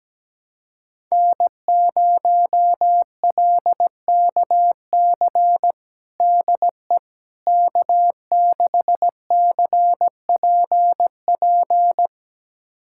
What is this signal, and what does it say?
Morse code, heard as a keyed tone, about 17 wpm, 700 Hz. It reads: N0LKC DE K6CPP
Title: USA Callsign DE Callsign - 5 Characters - 17wpm